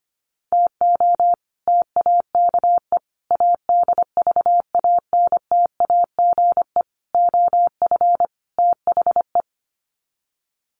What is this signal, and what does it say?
Morse code, heard as a keyed tone, about 25 wpm, 700 Hz. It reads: TO TAKE ADVANTAGE OF THE
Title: Sets of 5 Words 25wpm